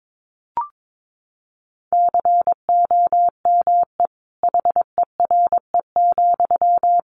Morse code, heard as a keyed tone, about 22 wpm, 700 Hz. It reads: COME HERE,
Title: Sentences from Top 1100 Words 22wpm